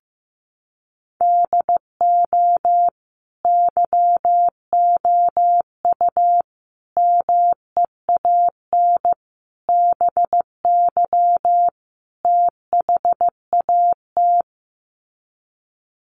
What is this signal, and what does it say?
Morse code, heard as a keyed tone, about 15 wpm, 700 Hz. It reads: DO YOU MEAN BY THAT
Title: Sets of 5 Words Encore2 15wpm